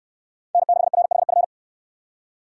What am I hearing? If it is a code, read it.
Morse code, heard as a keyed tone, about 69 wpm, 700 Hz. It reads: N6RHF